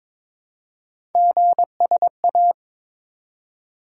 Morse code, heard as a keyed tone, about 23 wpm, 700 Hz. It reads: GSA
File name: practice / 22wpm / Three Letter-Number Combinations - 22wpm